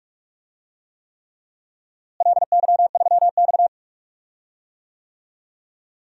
Morse code, heard as a keyed tone, about 46 wpm, 700 Hz. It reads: LY3X
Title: DXCC-100 Call Signs - 4 Characters - Repeated 3x - 45wpm